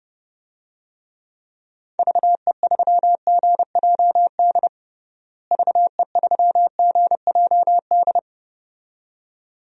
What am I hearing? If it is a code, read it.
Morse code, heard as a keyed tone, about 31 wpm, 700 Hz. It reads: VE3GJD VE3GJD
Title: DXCC-100 Call Signs - 6 Characters - Repeated 2x - 30wpm